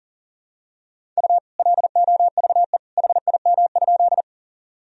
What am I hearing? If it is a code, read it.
Morse code, heard as a keyed tone, about 40 wpm, 700 Hz. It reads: U LOVE HIM?
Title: Sentences from Top 400 Words and Abbreviations 40wpm